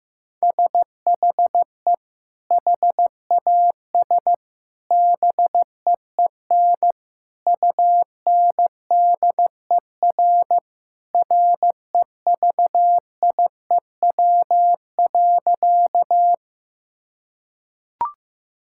Morse code, heard as a keyed tone, about 15 wpm, 700 Hz. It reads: SHE HAS BEEN UNDER REVIEW.
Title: Sentences from Top 1000 Words - Review Part 6 - 15wpm